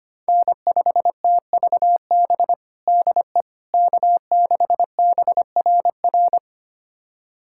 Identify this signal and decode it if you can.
Morse code, heard as a keyed tone, about 25 wpm, 700 Hz. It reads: N5TVB DE K6BRR
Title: USA Callsign DE Callsign - 5 Characters - 25wpm